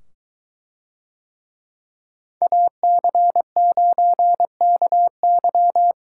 Morse code, heard as a keyed tone, about 23 wpm, 700 Hz. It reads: AC9KY